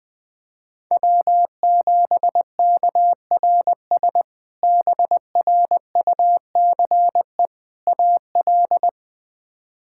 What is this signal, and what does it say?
Morse code, heard as a keyed tone, about 20 wpm, 700 Hz. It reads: W7KRS BRUCE AL
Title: K1USN SST QSO Practice - 20wpm